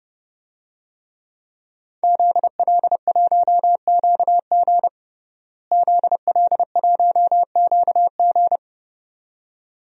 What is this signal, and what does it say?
Morse code, heard as a keyed tone, about 32 wpm, 700 Hz. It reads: ZL1QG ZL1QG
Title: DXCC-100 Call Signs - 5 Characters - Repeated 2x - 30wpm